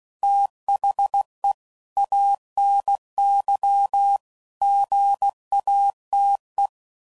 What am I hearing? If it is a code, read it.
Morse code, heard as a keyed tone, about 16 wpm, 800 Hz. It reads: THE ANY GATE
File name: CW Code Groups @15WPM / Technical Subjects / OR Gate